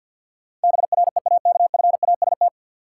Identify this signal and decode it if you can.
Morse code, heard as a keyed tone, about 53 wpm, 700 Hz. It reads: BREAKFAST
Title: Sentences from Top 1500 Words - New Word Drill Alt - 50wpm